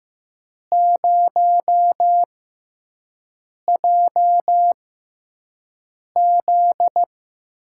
Morse code, heard as a keyed tone, about 15 wpm, spaced as for 5 wpm, 700 Hz. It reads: 0JZ
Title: USA Call Signs - 4 Characters - 6x Character Spacing - 15wpm